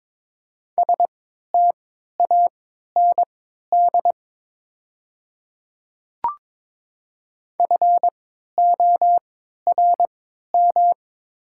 Morse code, heard as a keyed tone, about 22 wpm, spaced as for 12 wpm, 690 Hz. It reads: STAND FORM
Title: 3x Character Spacing - Top 500 Words - 22wpm